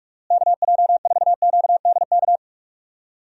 Morse code, heard as a keyed tone, about 42 wpm, 700 Hz. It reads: KJ4QDK
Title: USA Call Signs - 6 Characters - Repeated 3x - 45wpm